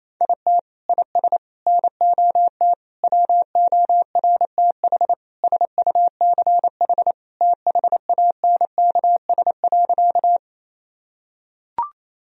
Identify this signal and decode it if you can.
Morse code, heard as a keyed tone, about 27 wpm, 700 Hz. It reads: IT IS NOT WORTH SUCH THANKS.
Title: Sentences from Top 1000 Words - Review Part 3 - 28wpm